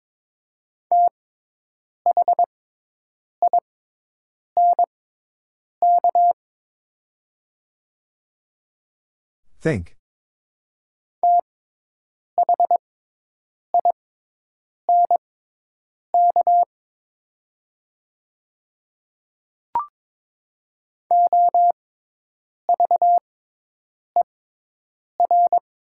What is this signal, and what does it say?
Morse code, heard as a keyed tone, about 22 wpm, spaced as for 8 wpm, 700 Hz. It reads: THINK THINK OVER
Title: 6x Character Spacing - Top 200 Words - 22wpm